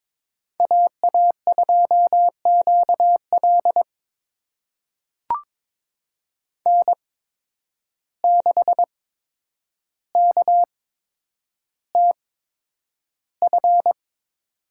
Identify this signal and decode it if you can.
Morse code, heard as a keyed tone, about 22 wpm, spaced as for 9 wpm, 700 Hz. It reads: AA2QL N6KTF
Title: USA Call Signs - 5 Characters - 8x Character Spacing - 22wpm